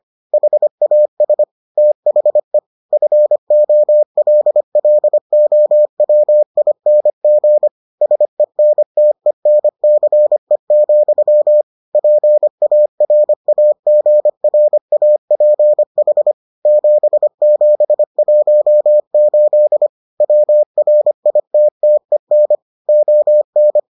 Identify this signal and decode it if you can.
Morse code, heard as a keyed tone, about 25 wpm, 600 Hz. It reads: HAS THE FOLLOWING SENTENCE, PARAGRAPH 7718 WRITTEN ON